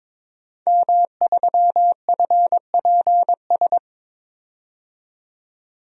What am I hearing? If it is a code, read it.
Morse code, heard as a keyed tone, about 22 wpm, 700 Hz. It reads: M3FPS
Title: DXCC-100 Call Signs - 5 Characters - 22wpm